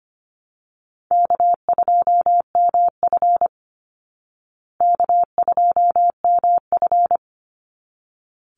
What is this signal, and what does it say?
Morse code, heard as a keyed tone, about 25 wpm, 700 Hz. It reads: K2MF K2MF
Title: Easy Call Signs Repeated 2x - 25wpm